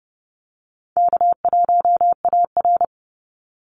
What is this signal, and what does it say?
Morse code, heard as a keyed tone, about 30 wpm, 700 Hz. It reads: K1AR
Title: Easy Call Signs - 30wpm